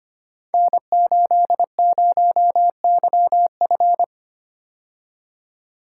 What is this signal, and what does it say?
Morse code, heard as a keyed tone, about 25 wpm, 700 Hz. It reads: N80YF
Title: Code Groups - 5 Letter:Numbers - 25wpm